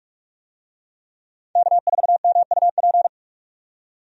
Morse code, heard as a keyed tone, about 46 wpm, 700 Hz. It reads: K4MUP
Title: USA Call Signs - 5 Characters - 45wpm